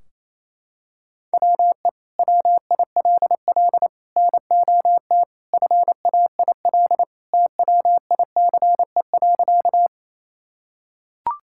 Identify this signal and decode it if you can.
Morse code, heard as a keyed tone, about 27 wpm, 700 Hz. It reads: WE WILL NOT FAIL TWICE.
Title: Sentences from Top 1600 Words 28wpm